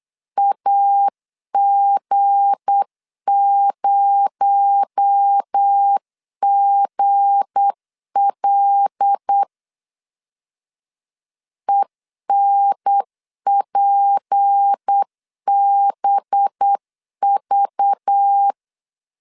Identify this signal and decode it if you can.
Morse code, heard as a keyed tone, about 8 wpm, 800 Hz. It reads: AG0GL ENPBV
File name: LEZIONI / 7 GF9Z / 7 G 8wpm 3118